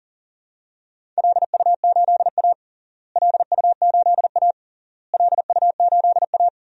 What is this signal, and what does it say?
Morse code, heard as a keyed tone, about 40 wpm, 700 Hz. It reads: LU8A LU8A LU8A
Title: DXCC-100 Call Signs - 4 Characters - Repeated 3x - 40wpm